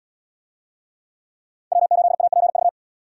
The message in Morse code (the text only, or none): W8APV